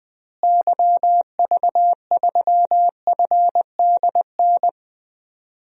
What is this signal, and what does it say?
Morse code, heard as a keyed tone, about 20 wpm, 700 Hz. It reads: YV3FDN
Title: DXCC-100 Call Signs - 6 Characters - Repeated 3x - 20wpm